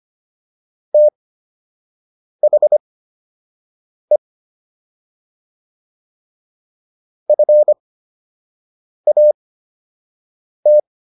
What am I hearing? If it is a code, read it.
Morse code, heard as a keyed tone, about 25 wpm, spaced as for 6 wpm, 600 Hz. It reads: THE FAT